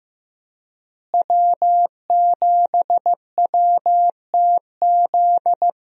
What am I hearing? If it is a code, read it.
Morse code, heard as a keyed tone, about 15 wpm, 700 Hz. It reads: W7WTZ
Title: USA Call Signs - 5 Characters - Repeated 3x - 15wpm